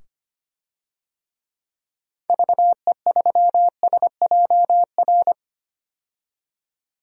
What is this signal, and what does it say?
Morse code, heard as a keyed tone, about 25 wpm, 700 Hz. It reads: VE3SJR
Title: DXCC-100 Call Signs - 6 Characters - 25wpm